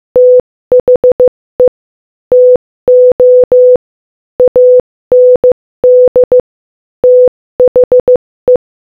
Morse code, heard as a keyed tone, about 15 wpm, 500 Hz. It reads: THE TO AND THE
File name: Morse-Code-Audio-Speed-15-10